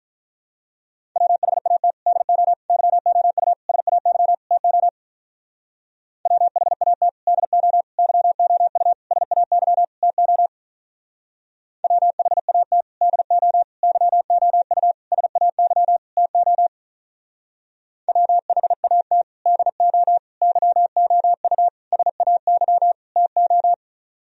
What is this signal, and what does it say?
Morse code, heard as a keyed tone, about 44 wpm, 700 Hz. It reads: WHAT DO YOU SAY TO WHAT DO YOU SAY TO WHAT DO YOU SAY TO WHAT DO YOU SAY TO